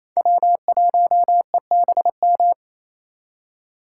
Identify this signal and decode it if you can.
Morse code, heard as a keyed tone, about 28 wpm, 700 Hz. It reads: W1EBM
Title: K1USN SST QSO Practice - 28wpm